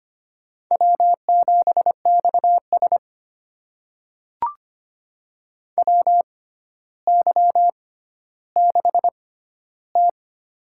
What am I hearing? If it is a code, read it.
Morse code, heard as a keyed tone, about 25 wpm, spaced as for 13 wpm, 700 Hz. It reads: W7XS WY6T
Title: USA Call Signs - 4 Characters - 6x Character Spacing - 25wpm